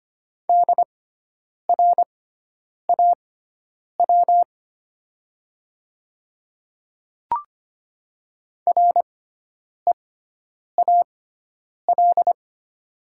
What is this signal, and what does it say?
Morse code, heard as a keyed tone, about 25 wpm, spaced as for 9 wpm, 700 Hz. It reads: DRAW REAL
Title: 6x Character Spacing - Top 300 Words - 25wpm